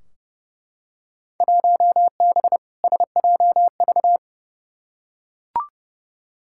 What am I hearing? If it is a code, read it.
Morse code, heard as a keyed tone, about 31 wpm, 700 Hz. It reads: 1B SJV
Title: Field Day Exchanges - 30wpm